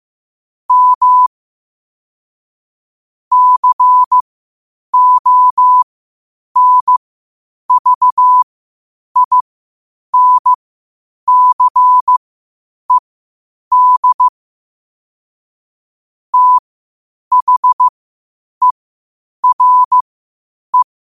Morse code, heard as a keyed tone, about 15 wpm, spaced as for 8 wpm, 1000 Hz. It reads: M CONVINCED THERE
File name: qotdfiles / 8wpm